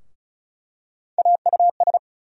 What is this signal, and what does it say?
Morse code, heard as a keyed tone, about 36 wpm, 700 Hz. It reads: AUS
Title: USA Call Sign Postfixes - 3 Letters 35wpm